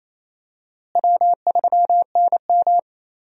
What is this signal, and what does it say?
Morse code, heard as a keyed tone, about 28 wpm, 700 Hz. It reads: W3NM
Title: USA Call Signs - 4 Characters - Repeated 3x - 28wpm